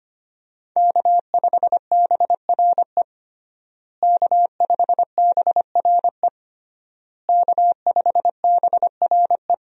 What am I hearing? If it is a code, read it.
Morse code, heard as a keyed tone, about 25 wpm, 700 Hz. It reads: K5BRE K5BRE K5BRE
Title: USA Call Signs - 5 Characters - Repeated 3x - 25wpm